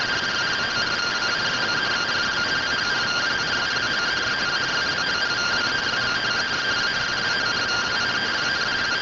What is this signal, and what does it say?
Two-tone teleprinter signal, about 170 Hz shift.